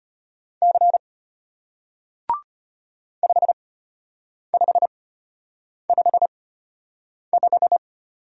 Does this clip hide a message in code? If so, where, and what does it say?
Yes: Morse code from 0.6 to 7.8 s, heard as a keyed tone, about 34 wpm, spaced as for 8 wpm, 700 Hz. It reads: C 5555